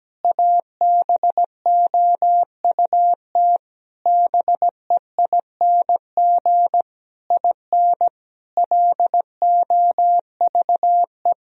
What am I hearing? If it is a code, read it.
Morse code, heard as a keyed tone, about 17 wpm, 700 Hz. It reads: ABOUT BEING IN LOVE
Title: Sentences from Top 1000 Words - Review Part 3 - 17wpm